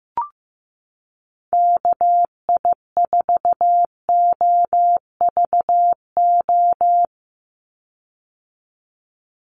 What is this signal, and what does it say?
Morse code, heard as a keyed tone, about 15 wpm, 700 Hz. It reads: KI4OVO
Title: USA Call Signs 15wpm